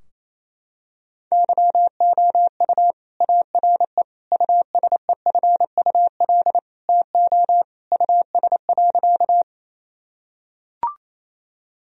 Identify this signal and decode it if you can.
Morse code, heard as a keyed tone, about 28 wpm, 700 Hz. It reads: YOU ARE USEFUL TO US.
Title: Sentences from Top 1000 Words - Review Part 4 - 28wpm